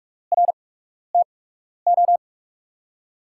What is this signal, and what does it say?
Morse code, heard as a keyed tone, about 42 wpm, spaced as for 12 wpm, 700 Hz. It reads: RTO
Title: USA Call Signs - 5 Characters - 8x Character Spacing - 45wpm